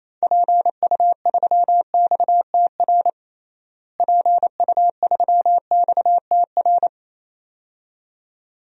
Morse code, heard as a keyed tone, about 28 wpm, 700 Hz. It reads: PU3XTR PU3XTR
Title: DXCC-100 Call Signs - 6 Characters - Repeated 2x - 28wpm